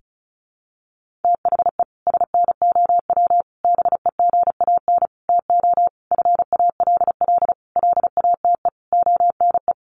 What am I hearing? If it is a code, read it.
Morse code, heard as a keyed tone, about 34 wpm, 700 Hz. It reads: THE SNOW BEGAN TO FALL LATE ONE
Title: Sentences from Top 1500 Words 35wpm